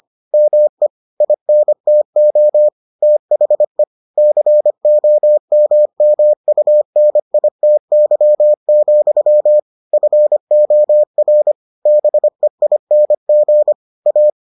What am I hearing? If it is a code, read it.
Morse code, heard as a keyed tone, about 25 wpm, 600 Hz. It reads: ME INTO THE COMMUNITY, FOR BEING A